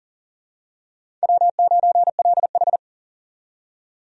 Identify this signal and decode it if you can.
Morse code, heard as a keyed tone, about 40 wpm, 700 Hz. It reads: W9LH
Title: USA Call Signs - 4 Characters - 40wpm